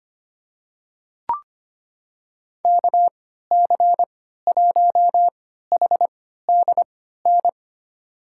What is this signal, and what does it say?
Morse code, heard as a keyed tone, about 25 wpm, spaced as for 14 wpm, 700 Hz. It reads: KC1HDN
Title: USA Call Signs - 6 Characters - 3x Character Spacing - 25wpm